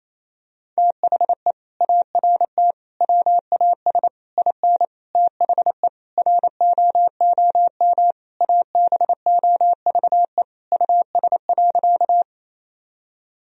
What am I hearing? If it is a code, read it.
Morse code, heard as a keyed tone, about 28 wpm, 700 Hz. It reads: THE ART WAS IN THE ROOM ABOVE US.
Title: Sentences from Top 1000 Words - Review Part 3 - 28wpm